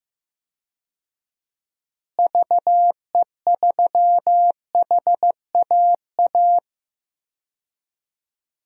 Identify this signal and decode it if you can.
Morse code, heard as a keyed tone, about 15 wpm, 700 Hz. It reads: VE3HAA
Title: DXCC-100 Call Signs - 6 Characters - Repeated 2x - 15wpm